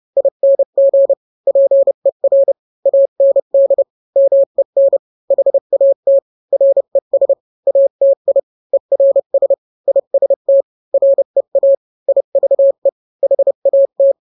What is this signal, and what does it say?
Morse code, heard as a keyed tone, about 30 wpm, 550 Hz. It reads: ING PER AND MEN HAT RES ATI ERS IST REA IVE HAT